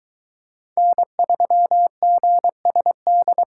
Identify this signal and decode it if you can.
Morse code, heard as a keyed tone, about 23 wpm, 700 Hz. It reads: N3GSD